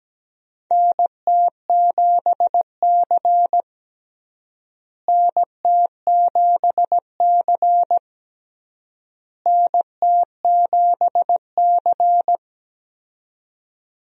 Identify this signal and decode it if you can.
Morse code, heard as a keyed tone, about 17 wpm, 700 Hz. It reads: NT7C NT7C NT7C
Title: USA Call Signs - 4 Characters - Repeated 3x - 17wpm